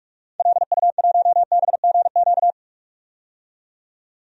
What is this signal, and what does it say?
Morse code, heard as a keyed tone, about 43 wpm, 700 Hz. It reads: LU1BGQ
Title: DXCC-100 Call Signs - 6 Characters - 45wpm